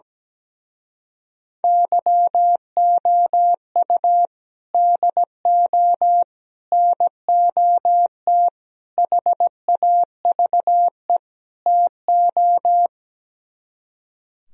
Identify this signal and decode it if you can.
Morse code, heard as a keyed tone, about 17 wpm, 700 Hz. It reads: YOU DO NOT HAVE TO
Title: Speed-Racing - Sets of 5 Words 17wpm